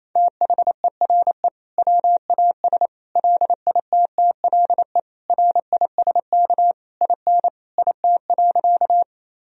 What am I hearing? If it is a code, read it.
Morse code, heard as a keyed tone, about 28 wpm, 700 Hz. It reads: THERE WAS LITTLE RISK IN IT.
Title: Sentences from Top 1000 Words - Review Part 4 - 28wpm